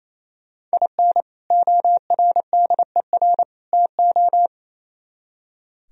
Morse code, heard as a keyed tone, about 35 wpm, 700 Hz. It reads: IN ORDER TO
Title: Speed-Racing - Sets of 3 Words 28wpm